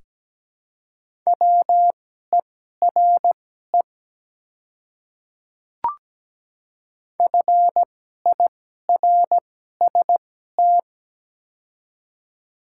Morse code, heard as a keyed tone, about 17 wpm, spaced as for 12 wpm, 690 Hz. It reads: WERE FIRST
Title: 2x Character Spacing - Top 100 Words - 17wpm